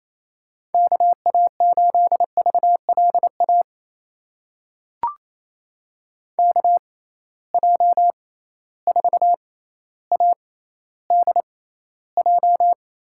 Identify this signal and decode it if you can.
Morse code, heard as a keyed tone, about 28 wpm, spaced as for 14 wpm, 700 Hz. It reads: KA8VLA KJ4ADJ
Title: USA Call Signs - 6 Characters - 6x Character Spacing - 28wpm